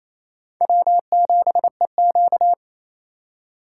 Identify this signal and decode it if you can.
Morse code, heard as a keyed tone, about 28 wpm, 700 Hz. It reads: W7EQ
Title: USA Call Signs - 4 Characters - 2x Character Spacing - 28wpm